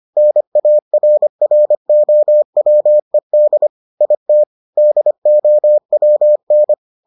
Morse code, heard as a keyed tone, about 25 wpm, 600 Hz. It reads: NARROWED IT DOWN